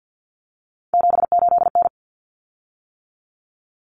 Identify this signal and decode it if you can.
Morse code, heard as a keyed tone, about 50 wpm, 700 Hz. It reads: 78N
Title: Common Cut Numbers - 3-Digit 50wpm